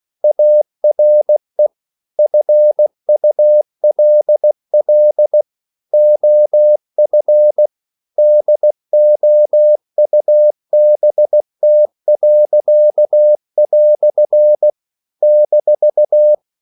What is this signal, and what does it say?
Morse code, heard as a keyed tone, about 16 wpm, 600 Hz. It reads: ARE FULL OF DOUBT." -
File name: CW / Practice / EnglishQuotations0116